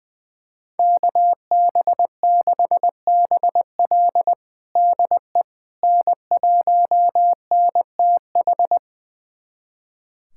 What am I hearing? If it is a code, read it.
Morse code, heard as a keyed tone, about 21 wpm, 700 Hz. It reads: KB6BL DE N1NTH
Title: USA Callsign DE Callsign - 5 Characters - 20wpm